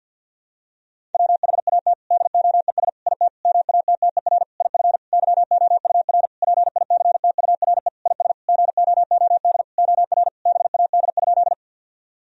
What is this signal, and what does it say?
Morse code, heard as a keyed tone, about 46 wpm, 700 Hz. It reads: WHAT DOES IT MATTER IF YOUR PICTURE IS GOOD OR BAD?
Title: Sentences from Top 1000 Words - Review Part 1 - 50wpm